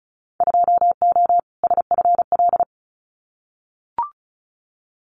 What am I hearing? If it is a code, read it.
Morse code, heard as a keyed tone, about 36 wpm, 700 Hz. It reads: 2O SFL
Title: Winter Field Day Exchanges 35wpm